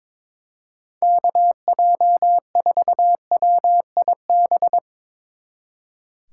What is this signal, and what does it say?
Morse code, heard as a keyed tone, about 24 wpm, 700 Hz. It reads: KJ4WIB
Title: Speed Racing - USA 2x3 Call Signs 22wpm